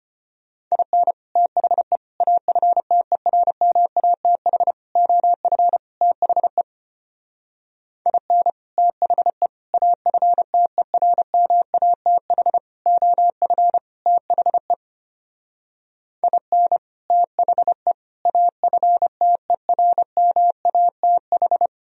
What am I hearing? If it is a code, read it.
Morse code, heard as a keyed tone, about 32 wpm, 700 Hz. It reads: IN THE AFTERMATH OF THE IN THE AFTERMATH OF THE IN THE AFTERMATH